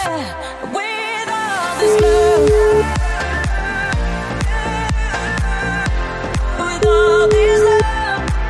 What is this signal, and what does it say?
Ringback tone (Europe, 425 Hz), with noise.